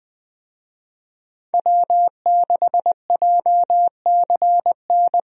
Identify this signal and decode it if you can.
Morse code, heard as a keyed tone, about 20 wpm, 700 Hz. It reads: W6JCN
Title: USA Call Signs - 5 Characters - Repeated 2x - 20wpm